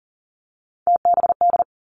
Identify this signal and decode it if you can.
Morse code, heard as a keyed tone, about 38 wpm, 700 Hz. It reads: TBD
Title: Three Challenging Letter-Number Combinations 40wpm